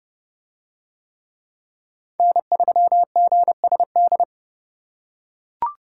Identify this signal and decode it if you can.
Morse code, heard as a keyed tone, about 29 wpm, 700 Hz. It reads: N3GSD